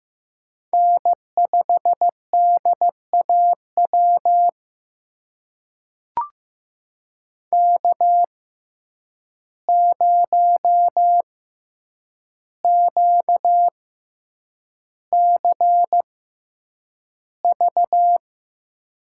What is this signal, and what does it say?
Morse code, heard as a keyed tone, about 15 wpm, spaced as for 8 wpm, 700 Hz. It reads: N5DAW K0QCV
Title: USA Call Signs - 5 Characters - 6x Character Spacing - 15wpm